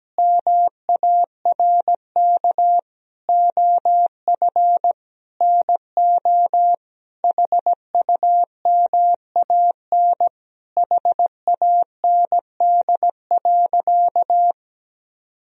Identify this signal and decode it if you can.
Morse code, heard as a keyed tone, about 17 wpm, 700 Hz. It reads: MARK OF NO HUMAN HAND.
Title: Sentences from Top 1000 Words - Review Part 3 - 17wpm